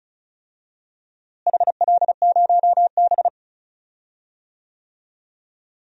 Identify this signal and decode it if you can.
Morse code, heard as a keyed tone, about 36 wpm, 700 Hz. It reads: HL0B
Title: DXCC-100 Call Signs - 4 Characters - Repeated 2x - 35wpm